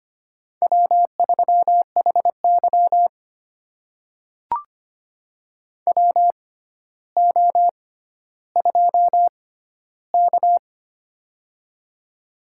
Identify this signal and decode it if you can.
Morse code, heard as a keyed tone, about 25 wpm, spaced as for 13 wpm, 700 Hz. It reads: W3HY WO2K
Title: USA Call Signs - 4 Characters - 6x Character Spacing - 25wpm